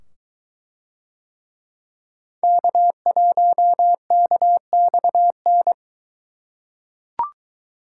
Morse code, heard as a keyed tone, about 23 wpm, 700 Hz. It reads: K1KXN